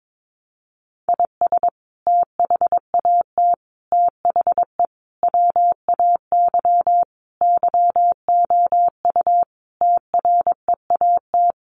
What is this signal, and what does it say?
Morse code, heard as a keyed tone, about 22 wpm, 700 Hz. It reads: IS THAT THE WAY YOU TREAT
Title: Sentences from Top 1300 Words 22wpm